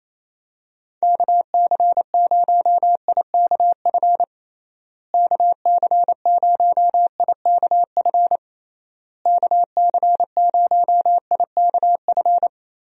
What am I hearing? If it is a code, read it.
Morse code, heard as a keyed tone, about 28 wpm, 700 Hz. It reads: KC0IKF KC0IKF KC0IKF
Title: USA Call Signs - 6 Characters - Repeated 3x - 28wpm